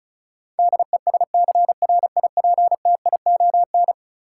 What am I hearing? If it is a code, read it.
Morse code, heard as a keyed tone, about 34 wpm, 700 Hz. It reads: DESCRIPTION
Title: Sentences from Top 1300 Words - New Word Drill Alt - 35wpm